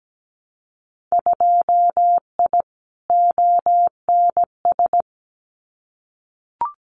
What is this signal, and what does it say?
Morse code, heard as a keyed tone, about 17 wpm, 700 Hz. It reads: 2I ONS
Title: Winter Field Day Exchanges Encore 17wpm